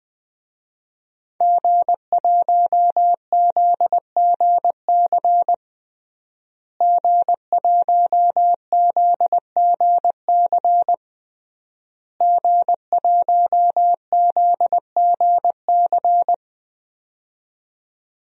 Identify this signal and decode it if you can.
Morse code, heard as a keyed tone, about 20 wpm, 700 Hz. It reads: G1ZGC G1ZGC G1ZGC
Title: DXCC-100 Call Signs - 5 Characters - Repeated 3x - 20wpm